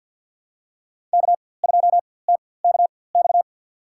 Morse code, heard as a keyed tone, about 49 wpm, spaced as for 23 wpm, 700 Hz. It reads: K2TKX
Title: USA Call Signs - 4x Character Spacing - 50wpm